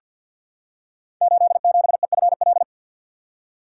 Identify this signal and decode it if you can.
Morse code, heard as a keyed tone, about 48 wpm, 700 Hz. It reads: 87EFL
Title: Code Groups - 5 Letter:Numbers - 50wpm